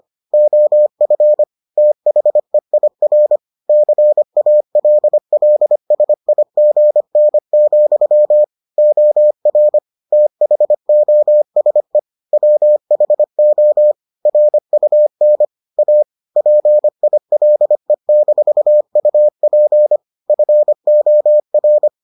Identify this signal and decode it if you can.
Morse code, heard as a keyed tone, about 25 wpm, 600 Hz. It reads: OF THEIR CALLSIGN, OR THOSE WHO RUN A PILE-UP FOR